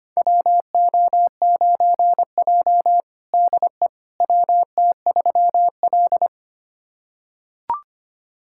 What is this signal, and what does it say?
Morse code, heard as a keyed tone, about 25 wpm, 700 Hz. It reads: WO9J DE WT3L
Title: USA Callsign DE Callsign - 4 Characters - 25wpm